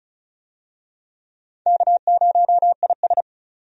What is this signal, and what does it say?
Morse code, heard as a keyed tone, about 36 wpm, 700 Hz. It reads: K0IS
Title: USA Call Signs - 4 Characters - 35wpm